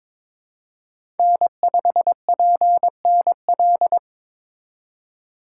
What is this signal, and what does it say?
Morse code, heard as a keyed tone, about 22 wpm, 700 Hz. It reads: N5PNL
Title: USA Call Signs - 4x Character Spacing - 22wpm